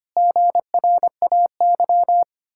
Morse code, heard as a keyed tone, about 25 wpm, 700 Hz. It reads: GRAY
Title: Pangram Challenge v2 - 25wpm